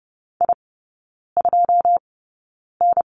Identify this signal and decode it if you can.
Morse code, heard as a keyed tone, about 30 wpm, spaced as for 9 wpm, 700 Hz. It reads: I2N
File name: 081 Lesson 14e - Number 2 - Three Characters 30wpm